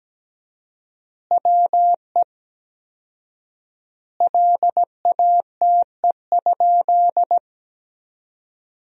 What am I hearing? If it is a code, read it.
Morse code, heard as a keyed tone, about 17 wpm, 700 Hz. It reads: WE LATE?
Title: 4x Word-Spacing - Sentences from Top 300 Words - 17wpm